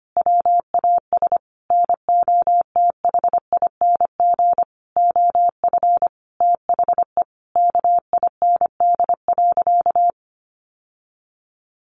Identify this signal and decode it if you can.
Morse code, heard as a keyed tone, about 25 wpm, 700 Hz. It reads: WAS NOTHING OF THE KIND.
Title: Sentences from Top 1200 Words 25wpm